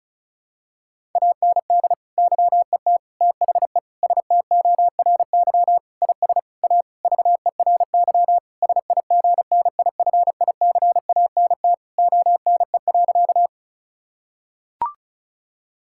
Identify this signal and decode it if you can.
Morse code, heard as a keyed tone, about 34 wpm, 700 Hz. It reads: AND YET THE STORY IS A VERY SIGNIFICANT ONE.
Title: Sentences from Top 1000 Words - Review Part 6 - 35wpm